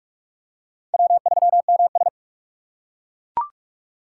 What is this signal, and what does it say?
Morse code, heard as a keyed tone, about 41 wpm, 700 Hz. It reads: W3MS